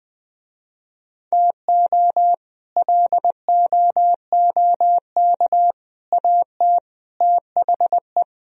Morse code, heard as a keyed tone, about 20 wpm, 700 Hz. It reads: TO LOOK AT THE
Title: Speed-Racing - Sets of 4 Words 17wpm